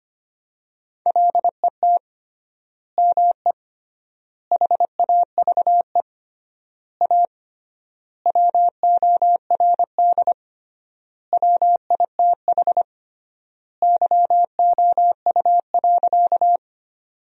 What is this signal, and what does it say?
Morse code, heard as a keyed tone, about 25 wpm, 700 Hz. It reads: LET ME HAVE A WORD WITH YOU.
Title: 3x Word-Spacing - Sentences from Top 300 Words - 25wpm